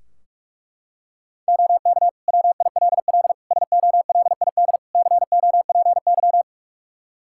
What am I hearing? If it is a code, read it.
Morse code, heard as a keyed tone, about 41 wpm, 700 Hz. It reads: OK WILL SOLID COPY
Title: Practice QSO - Exchange 3 - More Info - 45wpm